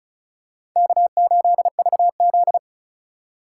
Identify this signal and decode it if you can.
Morse code, heard as a keyed tone, about 36 wpm, 700 Hz. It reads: K8VZ
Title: USA Call Signs - 8x Character Spacing - 35wpm